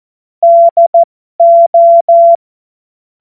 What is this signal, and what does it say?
Morse code, heard as a keyed tone, about 14 wpm, spaced as for 12 wpm, 680 Hz. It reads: DO
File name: Morsekurs / Witze